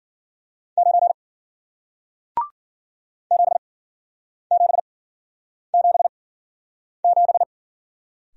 Morse code, heard as a keyed tone, about 54 wpm, spaced as for 9 wpm, 700 Hz. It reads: 9 7777